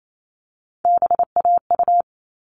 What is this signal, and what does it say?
Morse code, heard as a keyed tone, about 28 wpm, 700 Hz. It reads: BAU
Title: All Cut Numbers - 3-Digit 28wpm